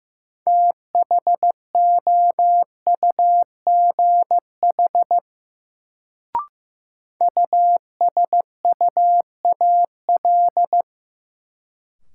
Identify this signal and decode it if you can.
Morse code, heard as a keyed tone, about 15 wpm, 700 Hz. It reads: THOUGH USUAL
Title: Sentences from Top 400 Words - New Word Drill - 15wpm